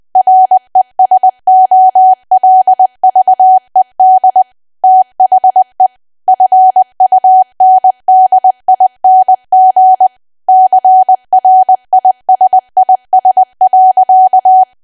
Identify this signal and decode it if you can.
Morse code, heard as a keyed tone, about 20 wpm, 750 Hz. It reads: RESOLVED THE FUNDING CRISIS.
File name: cwa / W1AW-Level3 / 20WPM